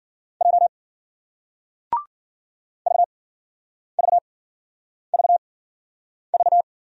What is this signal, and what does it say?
Morse code, heard as a keyed tone, about 55 wpm, spaced as for 9 wpm, 700 Hz. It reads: J VVVV